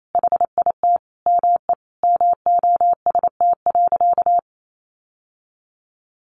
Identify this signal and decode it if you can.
Morse code, heard as a keyed tone, about 28 wpm, 700 Hz. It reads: HIT ME MOST.
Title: Sentences from Top 1400 Words 28wpm